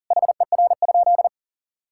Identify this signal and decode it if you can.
Morse code, heard as a keyed tone, about 40 wpm, 700 Hz. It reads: HER?
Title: Sentences from Top 1000 Words - Review Part 6 - 40wpm